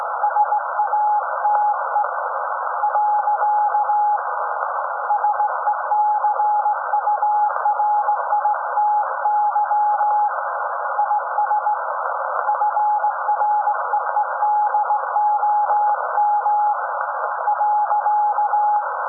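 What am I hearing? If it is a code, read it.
Morse code, heard as a keyed tone, about 18 wpm, 840 Hz, with noise. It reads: SATE . SECURITY IS FREEDOM .
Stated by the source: SAQ Grimeton Radio, Alexanderson 200 kW alternator on 17.2 kHz, received in Potsdam with a very strong signal on World Radio Day.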